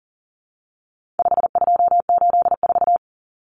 Morse code, heard as a keyed tone, about 40 wpm, 700 Hz. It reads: 5284
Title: Common Cut Numbers - 4-Digit 40wpm